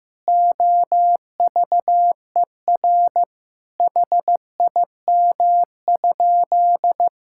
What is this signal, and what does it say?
Morse code, heard as a keyed tone, about 15 wpm, 700 Hz. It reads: OVER HIM?
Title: Sentences from Top 1600 Words 15wpm